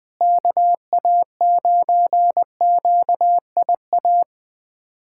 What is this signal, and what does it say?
Morse code, heard as a keyed tone, about 21 wpm, 700 Hz. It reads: KA9QIA
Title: USA Callsign DE Callsign 20wpm